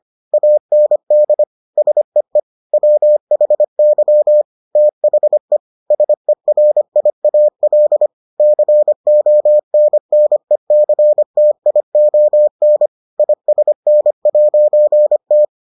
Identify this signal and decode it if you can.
Morse code, heard as a keyed tone, about 25 wpm, 600 Hz. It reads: AND SEE WHY THE SERIAL CONNECTION ISN'T